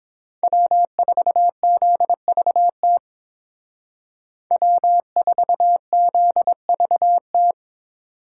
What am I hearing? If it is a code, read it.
Morse code, heard as a keyed tone, about 28 wpm, 700 Hz. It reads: W4ZVT W4ZVT